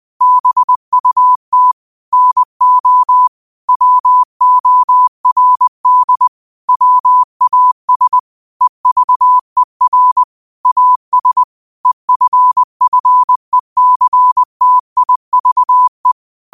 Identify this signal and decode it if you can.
Morse code, heard as a keyed tone, about 20 wpm, 1000 Hz. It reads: BUT NO WORD WAS EVER AS EFFECTIVE